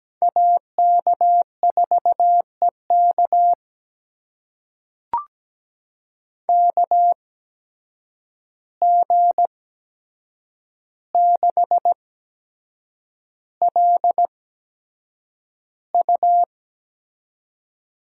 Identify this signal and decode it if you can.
Morse code, heard as a keyed tone, about 17 wpm, spaced as for 7 wpm, 700 Hz. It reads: AK4EK KG6LU
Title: USA Call Signs - 5 Characters - 8x Character Spacing - 17wpm